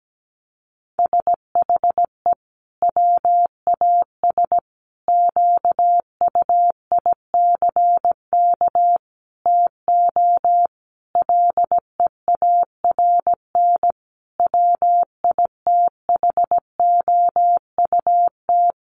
Morse code, heard as a keyed tone, about 17 wpm, 700 Hz. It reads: SHE WAS QUICK TO LEARN WITHOUT
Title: Sentences from Top 1200 Words 17wpm